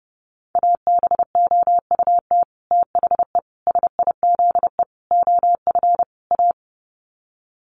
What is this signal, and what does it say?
Morse code, heard as a keyed tone, about 30 wpm, 700 Hz. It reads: ABOUT THE SIZE OF A
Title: Sets of 5 Words 30wpm